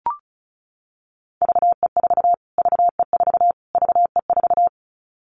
Morse code, heard as a keyed tone, about 34 wpm, 700 Hz. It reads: VE4 VE4 VE4
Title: Callsign Prefixes for Top 25 DXCC - Repeated 3x - 35wpm